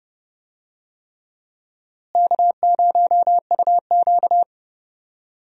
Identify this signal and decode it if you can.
Morse code, heard as a keyed tone, about 31 wpm, 700 Hz. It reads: K0UQ
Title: USA Call Signs - 4 Characters - Repeated 3x - 30wpm